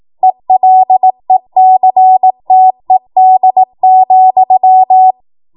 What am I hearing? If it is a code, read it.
Morse code, heard as a keyed tone, about 18 wpm, 750 Hz. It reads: ELECTED,
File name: Morse / 18WPM